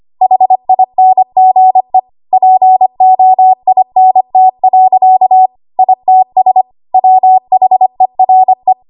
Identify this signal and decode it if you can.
Morse code, heard as a keyed tone, about 25 wpm, 750 Hz. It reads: HINGE POINT. ITS WHERE
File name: Morse / 25WPM